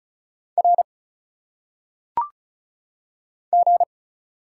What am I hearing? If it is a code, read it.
Morse code, heard as a keyed tone, about 35 wpm, 700 Hz. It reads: R G
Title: Single Letters - Phonetic Pronunciation 35wpm